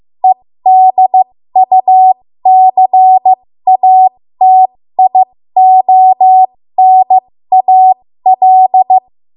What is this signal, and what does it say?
Morse code, heard as a keyed tone, about 15 wpm, spaced as for 13 wpm, 750 Hz. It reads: EDUCATIONAL
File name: Morse / 13WPM